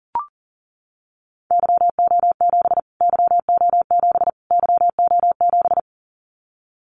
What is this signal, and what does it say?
Morse code, heard as a keyed tone, about 40 wpm, 700 Hz. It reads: YO7 YO7 YO7
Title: Callsign Prefixes for Top 25 DXCC - Repeated 3x - 40wpm